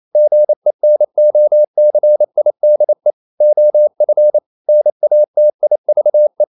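Morse code, heard as a keyed tone, about 28 wpm, 600 Hz. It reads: GENOCIDE OF NATIVE
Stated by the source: Morse code practice at 28 wpm, text of Ken Follett's novel Never in English, from CW med Gnister.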